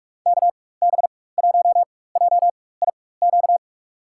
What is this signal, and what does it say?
Morse code, heard as a keyed tone, about 44 wpm, spaced as for 21 wpm, 700 Hz. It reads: KB1JIQ
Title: USA Call Signs - 6 Characters - 4x Character Spacing - 45wpm